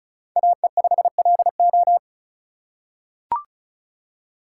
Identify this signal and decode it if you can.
Morse code, heard as a keyed tone, about 36 wpm, 700 Hz. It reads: AE5LO
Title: USA Call Signs - 5 Characters - 4x Character Spacing - 35wpm